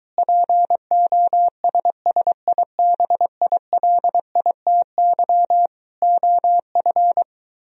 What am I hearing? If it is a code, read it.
Morse code, heard as a keyed tone, about 23 wpm, 700 Hz. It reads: POSSIBILITY OF